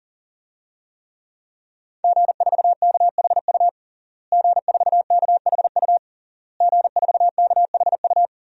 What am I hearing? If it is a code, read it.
Morse code, heard as a keyed tone, about 40 wpm, 700 Hz. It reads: G4KHU G4KHU G4KHU
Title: DXCC-100 Call Signs - 5 Characters - Repeated 3x - 40wpm